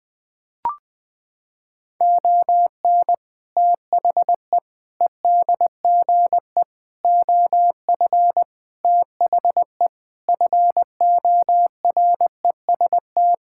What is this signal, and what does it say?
Morse code, heard as a keyed tone, about 20 wpm, 700 Hz. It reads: ON THE EDGE OF THE FOREST
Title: Sentences with 4-Word Phrases - 20wpm